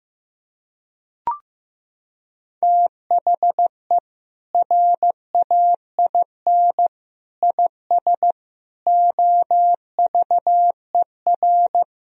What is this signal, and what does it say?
Morse code, heard as a keyed tone, about 15 wpm, 700 Hz. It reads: THE RAIN IS OVER
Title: Sentences from Top 1000 Words - Review Part 1 - 15wpm